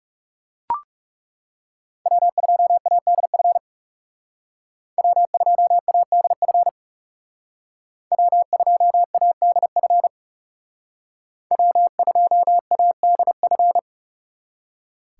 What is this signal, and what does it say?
Morse code, heard as a keyed tone, about 39 wpm, 700 Hz. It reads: W2ADF W2ADF W2ADF W2ADF